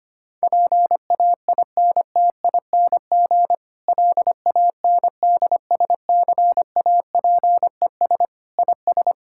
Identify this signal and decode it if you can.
Morse code, heard as a keyed tone, about 25 wpm, 700 Hz. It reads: PAINTING LANDSCAPES IS
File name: practice / 25wpm / Difficult Sentences with Hints - 25wpm